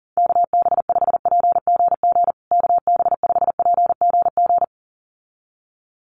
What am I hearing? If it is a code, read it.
Morse code, heard as a keyed tone, about 40 wpm, 700 Hz. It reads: KB5PGG KB5PGG
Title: USA Call Signs Repeated 2x 40wpm